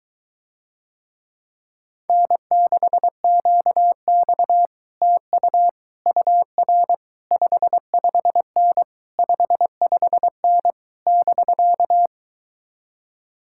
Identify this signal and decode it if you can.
Morse code, heard as a keyed tone, about 23 wpm, 700 Hz. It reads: N6QX TU UR 55N 55N <BK>